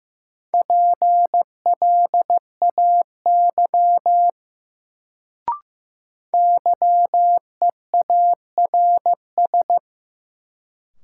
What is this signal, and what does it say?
Morse code, heard as a keyed tone, about 15 wpm, 700 Hz. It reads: PLAY YEARS
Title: Sentences from Top 200 Words - New Word Drill - 15wpm